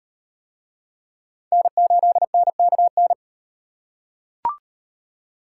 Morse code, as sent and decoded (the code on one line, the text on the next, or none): -. ---.. -. -.- -.
N8NKN